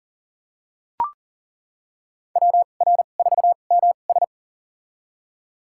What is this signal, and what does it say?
Morse code, heard as a keyed tone, about 40 wpm, spaced as for 29 wpm, 700 Hz. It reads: WR4MS
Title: USA Call Signs - 5 Characters - 2x Character Spacing - 40wpm